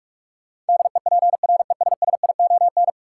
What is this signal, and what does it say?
Morse code, heard as a keyed tone, about 43 wpm, 700 Hz. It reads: DEPRESSION
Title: Sentences from Top 1300 Words - New Word Drill - 45wpm